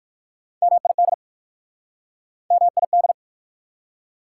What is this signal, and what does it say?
Morse code, heard as a keyed tone, about 68 wpm, spaced as for 58 wpm, 690 Hz. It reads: MID MID